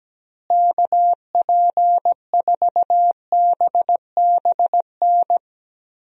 Morse code, heard as a keyed tone, about 17 wpm, 700 Hz. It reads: KP4BBN
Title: DXCC-100 Call Signs - 6 Characters - Repeated 2x - 17wpm